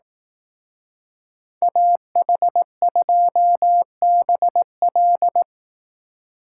Morse code, heard as a keyed tone, about 18 wpm, 700 Hz. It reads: AH2BL
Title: Speed Racing - USA Call Signs 15wpm